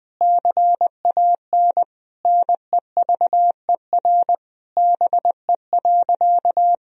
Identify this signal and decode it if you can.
Morse code, heard as a keyed tone, about 21 wpm, 700 Hz. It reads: CAN NEVER BE.
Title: Sentences from Top 300 Words and Abbreviations 20wpm